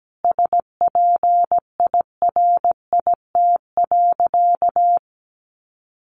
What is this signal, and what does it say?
Morse code, heard as a keyed tone, about 17 wpm, 700 Hz. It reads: SPIRIT.
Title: Sentences from Top 1400 Words 17wpm